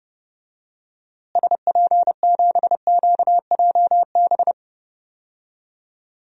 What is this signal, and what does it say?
Morse code, heard as a keyed tone, about 30 wpm, 700 Hz. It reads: SP7QJB
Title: DXCC-100 Call Signs - 6 Characters - Repeated 3x - 30wpm